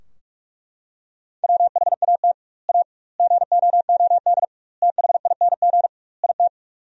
Morse code, heard as a keyed tone, about 41 wpm, 700 Hz. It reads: WHAT A GOOD THING IT
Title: 2x Word-Spacing - Sentences from Top 200 Words - 45wpm